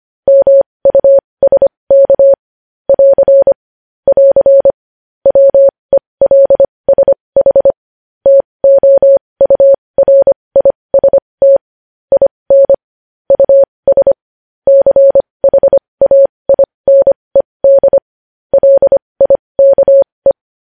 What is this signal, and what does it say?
Morse code, heard as a keyed tone, about 25 wpm, spaced as for 20 wpm, 550 Hz. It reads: MUSK <AR> <AR> WELSH TOURIST IN US CHAINED LIKE
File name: podcast / News headlines-1x-25x20